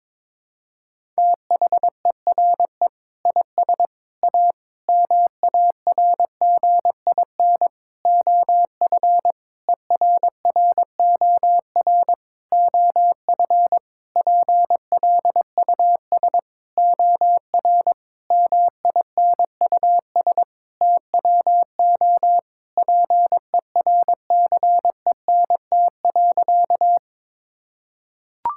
Morse code, heard as a keyed tone, about 22 wpm, 700 Hz. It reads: THERE IS A MARGIN OF ERROR OF PLUS OR MINUS TWO PERCENT.
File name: practice / 22wpm / Sentences with 8-Word Phrases - 22wpm